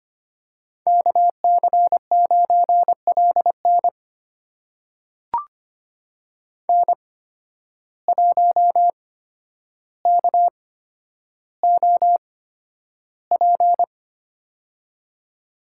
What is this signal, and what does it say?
Morse code, heard as a keyed tone, about 25 wpm, spaced as for 11 wpm, 700 Hz. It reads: KC9LN N1KOP
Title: USA Call Signs - 8x Character Spacing - 25wpm